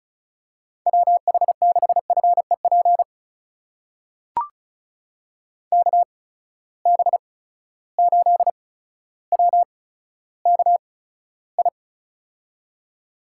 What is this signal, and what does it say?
Morse code, heard as a keyed tone, about 36 wpm, spaced as for 15 wpm, 700 Hz. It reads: WH6FEP KB8WKI
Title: USA Call Signs - 6 Characters - 8x Character Spacing - 35wpm